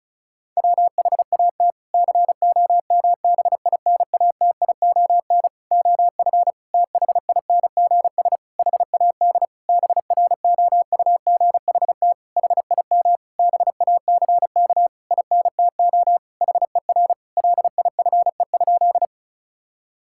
Morse code, heard as a keyed tone, about 34 wpm, 700 Hz. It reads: WHAT COMBINATION OF THINGS HAD BROUGHT HIM BACK INTO HER LIFE?
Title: Sentences from Top 1000 Words - Review Part 6 - 35wpm